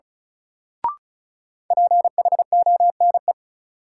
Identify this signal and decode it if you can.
Morse code, heard as a keyed tone, about 34 wpm, 700 Hz. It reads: PHONE
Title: Sentences from Top 1100 Words - New Word Drill - 35wpm